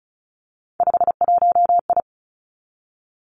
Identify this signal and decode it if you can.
Morse code, heard as a keyed tone, about 36 wpm, 700 Hz. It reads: 51I
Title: Three Challenging Letter-Number Combinations 35wpm